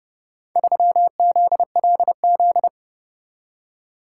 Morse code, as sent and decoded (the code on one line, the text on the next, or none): ...-- --.. .-.. --..
3ZLZ